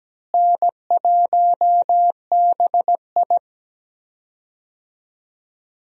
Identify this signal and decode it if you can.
Morse code, heard as a keyed tone, about 17 wpm, 700 Hz. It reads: N1BI
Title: USA Call Signs - 4 Characters - Repeated 2x - 17wpm